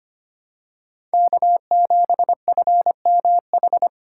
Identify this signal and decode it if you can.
Morse code, heard as a keyed tone, about 25 wpm, 700 Hz. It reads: K7FMH
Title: USA Call Signs - 5 Characters - 6x Character Spacing - 25wpm